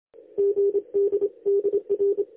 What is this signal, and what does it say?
Morse code, heard as a keyed tone, about 27 wpm, 400 Hz. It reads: GD DR
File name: lx1nj- -1005utc-10110khz2-49235